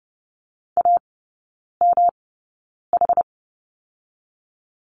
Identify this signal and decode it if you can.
Morse code, heard as a keyed tone, about 30 wpm, spaced as for 9 wpm, 700 Hz. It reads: AMH
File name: Lesson 18e - Letter m - Three Characters 30wpm